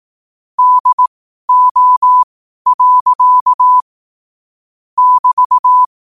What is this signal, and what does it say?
Morse code, heard as a keyed tone, about 18 wpm, spaced as for 12 wpm, 1000 Hz. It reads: DO. <BT>
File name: qotdfiles / 12wpm